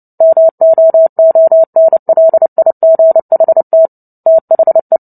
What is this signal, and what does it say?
Morse code, heard as a keyed tone, about 30 wpm, 650 Hz. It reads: MOONLIGHT THE
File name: Book Two / 03-Chapter 3 - 29 WPM